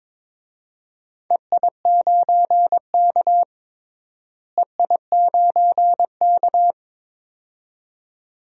Morse code, heard as a keyed tone, about 22 wpm, 700 Hz. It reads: EI9K EI9K
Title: DXCC-100 Call Signs - 4 Characters - Repeated 2x - 22wpm